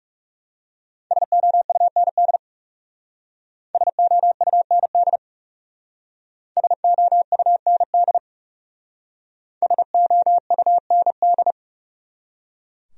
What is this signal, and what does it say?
Morse code, heard as a keyed tone, about 37 wpm, 700 Hz. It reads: SOUND SOUND SOUND SOUND
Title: Speed-Racing Top 200 Words 30wpm